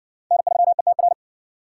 Morse code, heard as a keyed tone, about 58 wpm, 700 Hz. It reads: N3ETR